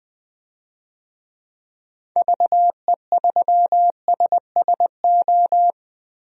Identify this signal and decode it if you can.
Morse code, heard as a keyed tone, about 21 wpm, 700 Hz. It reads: VE3SSO
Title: DXCC-100 Call Signs - 6 Characters - Repeated 3x - 20wpm